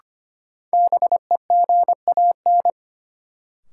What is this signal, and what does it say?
Morse code, heard as a keyed tone, about 25 wpm, 700 Hz. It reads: BEGAN
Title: Sentences from Top 300 Words - New Word Drill - 25wpm